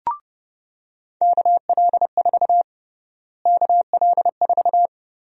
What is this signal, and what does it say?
Morse code, heard as a keyed tone, about 32 wpm, 700 Hz. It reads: KL4 KL4
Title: USA Call Sign Prefixes - Repeated 2x - 30wpm